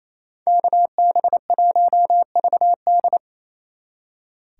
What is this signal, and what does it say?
Morse code, heard as a keyed tone, about 33 wpm, 700 Hz. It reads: KB1VD
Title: Speed Racing - USA 2x2 Call Signs 28wpm